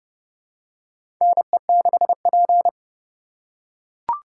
Morse code, heard as a keyed tone, about 30 wpm, 700 Hz. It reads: NE6P
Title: USA Call Signs - 4 Characters - 2x Character Spacing - 30wpm